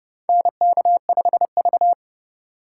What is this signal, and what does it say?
Morse code, heard as a keyed tone, about 29 wpm, 690 Hz. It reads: NK5V
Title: USA Call Signs - 4 Characters - 2x Character Spacing - 30wpm